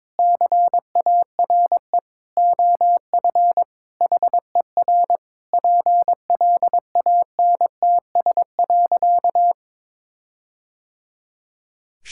Morse code, heard as a keyed tone, about 22 wpm, 700 Hz. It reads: CARE OF HER PLANTS.
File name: practice / 22wpm / Sentences with 4-Word Phrases - 22wpm